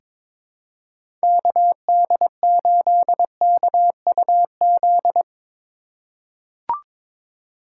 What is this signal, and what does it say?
Morse code, heard as a keyed tone, about 22 wpm, 700 Hz. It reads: KD8KUZ
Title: USA Call Signs - 6 Characters - 2x Character Spacing - 22wpm